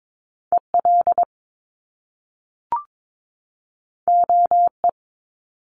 Morse code, heard as a keyed tone, about 22 wpm, 700 Hz. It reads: EL OE
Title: Top Two Letter Combinations in Words 22wpm